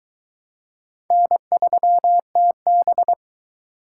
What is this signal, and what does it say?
Morse code, heard as a keyed tone, about 22 wpm, 700 Hz. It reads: N3TB